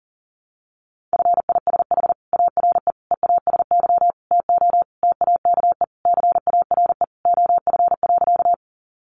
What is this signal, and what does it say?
Morse code, heard as a keyed tone, about 40 wpm, 700 Hz. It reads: FISH ARE EASY TO TAKE CARE OF.
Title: Easy Phrases 40wpm